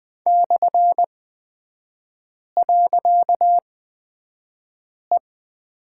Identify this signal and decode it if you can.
Morse code, heard as a keyed tone, about 20 wpm, spaced as for 5 wpm, 700 Hz. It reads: /.E
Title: 255 Lesson 41e - Period - Three Characters 20wpm